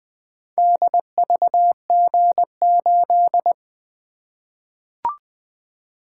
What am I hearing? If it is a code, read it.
Morse code, heard as a keyed tone, about 20 wpm, 700 Hz. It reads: DVG8
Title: Four Letter-Number Combinations - 20wpm